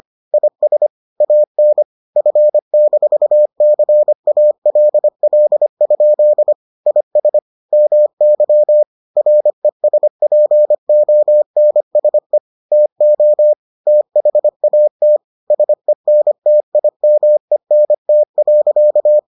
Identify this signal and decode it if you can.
Morse code, heard as a keyed tone, about 25 wpm, 600 Hz. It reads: IS AN F-CALL? IS MY RESPONSE TO THAT SENTIMENT.